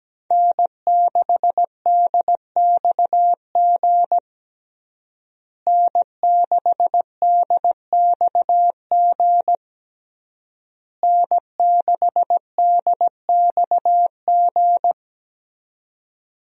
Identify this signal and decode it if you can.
Morse code, heard as a keyed tone, about 17 wpm, 700 Hz. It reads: N6DXG N6DXG N6DXG
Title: USA Call Signs - 5 Characters - Repeated 3x - 17wpm